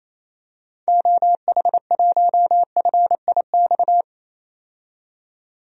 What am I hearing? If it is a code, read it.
Morse code, heard as a keyed tone, about 28 wpm, 700 Hz. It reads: OH1FIX
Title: DXCC-100 Call Signs - 6 Characters - 28wpm